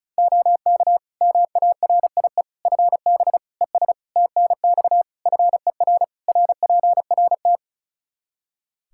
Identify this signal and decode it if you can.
Morse code, heard as a keyed tone, about 36 wpm, 700 Hz. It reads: OK MARIE FB ES TNX FER RPRT
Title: Practice QSO - Exchange 2 - Info - 35wpm